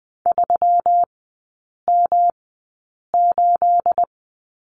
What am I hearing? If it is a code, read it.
Morse code, heard as a keyed tone, about 20 wpm, spaced as for 8 wpm, 700 Hz. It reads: 3M8
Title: 212 Lesson 34e - Number 8 - Three Characters 20wpm